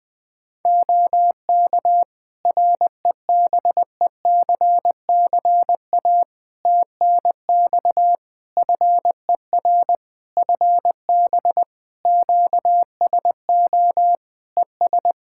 Morse code, heard as a keyed tone, about 20 wpm, 700 Hz. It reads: OK REBECCA TNX FER FB QSO ES
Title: Practice QSO - Exchange 4 - Ending - 20wpm